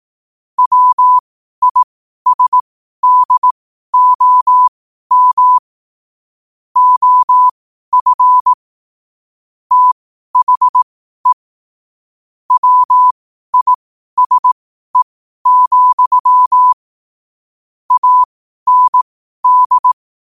Morse code, heard as a keyed tone, about 18 wpm, spaced as for 12 wpm, 1000 Hz. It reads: WISDOM OF THE WISE, AND
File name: qotdfiles / 12wpm